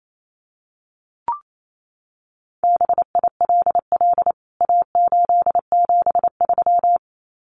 Morse code, heard as a keyed tone, about 28 wpm, 700 Hz. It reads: BILL A873
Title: CWT Exchange 28wpm